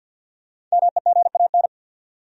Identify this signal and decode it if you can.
Morse code, heard as a keyed tone, about 48 wpm, 700 Hz. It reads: MEGAN
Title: USA Common Names - Max 5 Letters - 50wpm